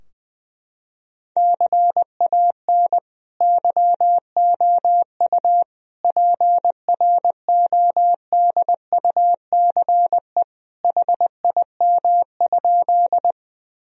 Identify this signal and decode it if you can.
Morse code, heard as a keyed tone, about 21 wpm, 700 Hz. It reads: CAN YOU PRODUCE HIM?
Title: Sentences from Top 1000 Words - Review Part 3 - 20wpm